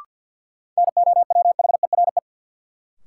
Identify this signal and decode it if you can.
Morse code, heard as a keyed tone, about 51 wpm, 700 Hz. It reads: NOWHERE
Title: Sets of 5 Words Encore - New Word Drill - 50wpm